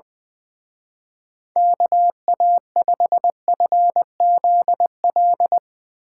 Morse code, heard as a keyed tone, about 22 wpm, 700 Hz. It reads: KA5FZL
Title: Speed Racing - USA 2x3 Call Signs 20wpm